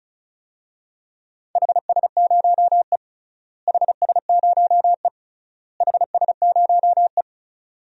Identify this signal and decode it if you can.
Morse code, heard as a keyed tone, about 36 wpm, 700 Hz. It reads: HS0E HS0E HS0E
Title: DXCC-100 Call Signs - 4 Characters - Repeated 3x - 35wpm